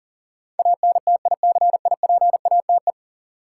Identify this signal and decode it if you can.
Morse code, heard as a keyed tone, about 38 wpm, 700 Hz. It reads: ANTICIPATE
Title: Sentences from Top 1500 Words - New Word Drill Alt - 40wpm